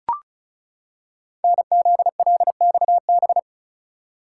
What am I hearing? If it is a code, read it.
Morse code, heard as a keyed tone, about 36 wpm, 690 Hz. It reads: NZLXB
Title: Code Groups - 5 Letters - 35wpm